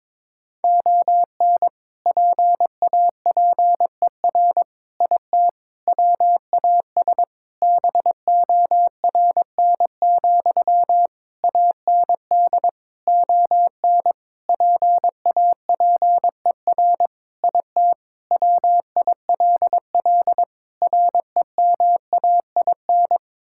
Morse code, heard as a keyed tone, about 22 wpm, 700 Hz. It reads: ON PAPER IT WAS BORN, AND ON PAPER IT WILL REMAIN
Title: Sentences from Top 1600 Words 22wpm